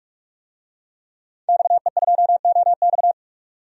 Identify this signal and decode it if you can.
Morse code, heard as a keyed tone, about 46 wpm, 700 Hz. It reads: XE2OX
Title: DXCC-100 Call Signs - 5 Characters - 45wpm